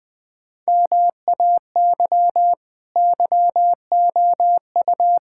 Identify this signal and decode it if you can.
Morse code, heard as a keyed tone, about 20 wpm, 700 Hz. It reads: MAY YOU
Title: Sentences from Top 1000 Words - Review Part 6 - 20wpm